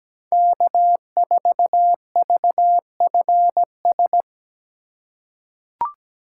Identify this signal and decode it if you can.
Morse code, heard as a keyed tone, about 17 wpm, 700 Hz. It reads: K4VFS
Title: USA Call Signs - 5 Characters - 8x Character Spacing - 17wpm